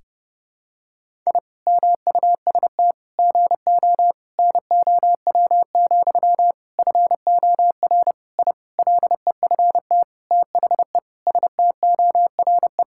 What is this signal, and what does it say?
Morse code, heard as a keyed tone, about 30 wpm, 700 Hz. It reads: I MUST GO NOW, FOR I LEFT THE STORE
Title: Sentences from Top 1000 Words - Review Part 5 - 30wpm